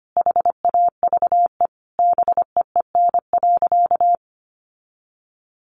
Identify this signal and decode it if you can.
Morse code, heard as a keyed tone, about 25 wpm, 700 Hz. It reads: HAVE BEEN.
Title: Sentences from Top 1000 Words 25wpm